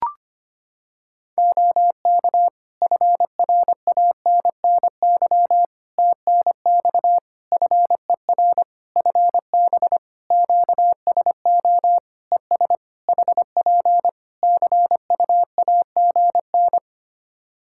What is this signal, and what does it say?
Morse code, heard as a keyed tone, about 25 wpm, 700 Hz. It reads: OK FRANNY TNX FER FB QSO ES HP CUAGN
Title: Practice QSO - Exchange 4 - Ending - 25wpm